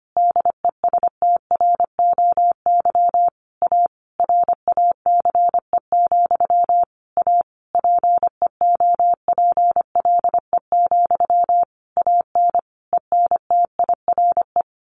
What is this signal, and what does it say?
Morse code, heard as a keyed tone, about 25 wpm, 700 Hz. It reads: DESTROY A RACE, A PEOPLE, AN ENTIRE
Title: A Sound Of Thunder 25wpm